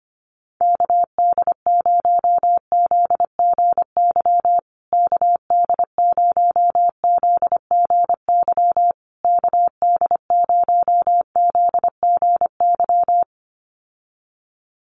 Morse code, heard as a keyed tone, about 25 wpm, 700 Hz. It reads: KD0ZGY KD0ZGY KD0ZGY
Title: USA Call Signs Repeated 3x 25wpm